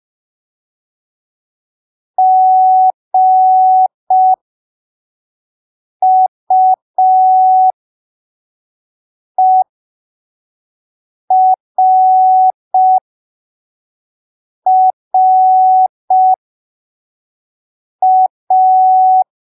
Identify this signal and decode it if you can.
Morse code, heard as a keyed tone, about 5 wpm, spaced as for 3 wpm, 750 Hz. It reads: GUERRA
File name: arquivos / texto / texto 14 05ppm